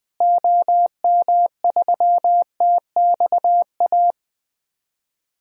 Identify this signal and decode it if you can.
Morse code, heard as a keyed tone, about 20 wpm, 700 Hz. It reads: OM3TXA
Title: DXCC-100 Call Signs - 6 Characters - Repeated 2x - 20wpm